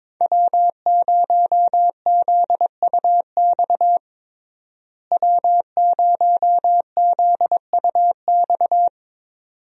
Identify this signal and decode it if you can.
Morse code, heard as a keyed tone, about 22 wpm, 700 Hz. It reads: W0ZUX W0ZUX
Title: USA Call Signs - 5 Characters - Repeated 2x - 22wpm